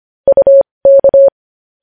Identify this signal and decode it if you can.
Morse code, heard as a keyed tone, about 25 wpm, spaced as for 20 wpm, 550 Hz. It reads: UK
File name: podcast / News headlines-1x-25x20